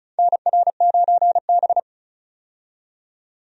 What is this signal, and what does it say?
Morse code, heard as a keyed tone, about 36 wpm, 700 Hz. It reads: NR9B
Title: USA Call Signs - 4 Characters - 35wpm